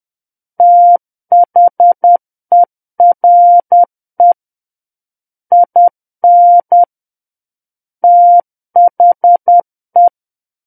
Morse code, heard as a keyed tone, about 10 wpm, 700 Hz. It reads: THERE IN THE
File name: Book One / 01-Chapter 1 - 10 WPM